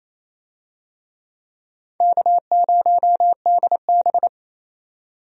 Morse code, heard as a keyed tone, about 28 wpm, 700 Hz. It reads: K0DB
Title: USA Call Signs - 4 Characters - 28wpm